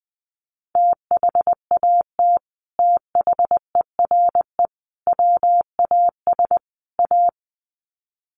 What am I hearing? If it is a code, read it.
Morse code, heard as a keyed tone, about 20 wpm, 700 Hz. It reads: THAT THERE WAS A
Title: Sets of 4 Words 20wpm